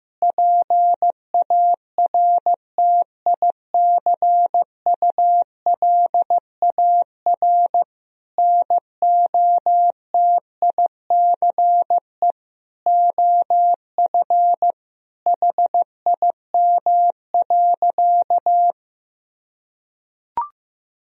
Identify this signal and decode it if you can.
Morse code, heard as a keyed tone, about 15 wpm, 700 Hz. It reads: PARTICULAR NOTICE OF HIM.
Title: Sentences from Top 1000 Words - Review Part 4 - 15wpm